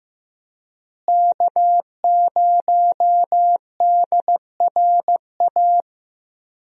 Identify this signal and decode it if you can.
Morse code, heard as a keyed tone, about 15 wpm, 700 Hz. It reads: K0DRA
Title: USA Call Signs - 5 Characters - Repeated 2x - 15wpm